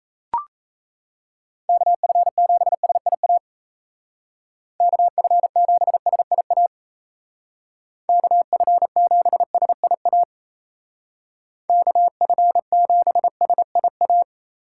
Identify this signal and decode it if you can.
Morse code, heard as a keyed tone, about 33 wpm, 700 Hz. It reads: KF7SIA KF7SIA KF7SIA KF7SIA
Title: Speed Racing - USA 2x3 Call Signs 28wpm